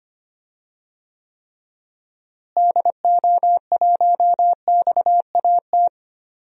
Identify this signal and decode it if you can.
Morse code, heard as a keyed tone, about 25 wpm, 700 Hz. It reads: DO1XAT
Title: DXCC-100 Call Signs - 6 Characters - Repeated 2x - 25wpm